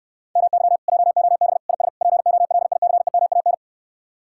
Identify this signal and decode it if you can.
Morse code, heard as a keyed tone, about 68 wpm, 700 Hz. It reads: MY JOB IS POLICEMAN